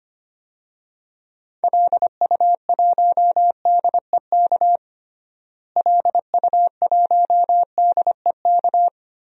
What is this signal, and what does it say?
Morse code, heard as a keyed tone, about 25 wpm, 700 Hz. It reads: LU1DEK LU1DEK
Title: DXCC-100 Call Signs - 6 Characters - Repeated 3x - 25wpm